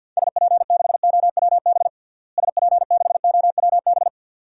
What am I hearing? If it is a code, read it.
Morse code, heard as a keyed tone, about 52 wpm, 700 Hz. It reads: SP6OWB SP6OWB
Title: DXCC-100 Call Signs - 6 Characters - Repeated 2x - 50wpm